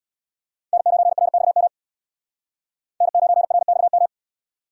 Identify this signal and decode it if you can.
Morse code, heard as a keyed tone, about 62 wpm, 700 Hz. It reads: N9RXM N9RXM
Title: Speed Racing - USA 1x3 Call Signs 50wpm